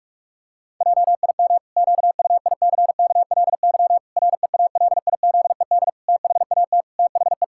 Morse code, heard as a keyed tone, about 46 wpm, 700 Hz. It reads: JIM QUICKLY REALIZED THAT THE
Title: Pangram Challenge v2 - 45wpm